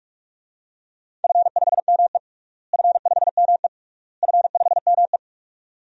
Morse code, heard as a keyed tone, about 41 wpm, 700 Hz. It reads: F5ME F5ME F5ME
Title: DXCC-100 Call Signs - 4 Characters - Repeated 3x - 45wpm